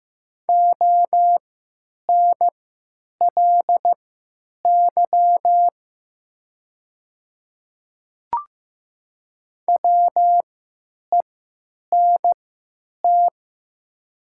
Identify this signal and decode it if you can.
Morse code, heard as a keyed tone, about 15 wpm, spaced as for 9 wpm, 700 Hz. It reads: ONLY WENT
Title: 3x Character Spacing - Top 300 Words - 15wpm